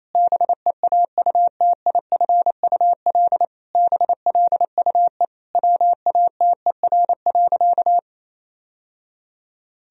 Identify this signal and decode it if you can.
Morse code, heard as a keyed tone, about 28 wpm, 700 Hz. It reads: BEAUTIFUL BLUE WATER.
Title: Sentences from Top 1000 Words - Review Part 3 - 28wpm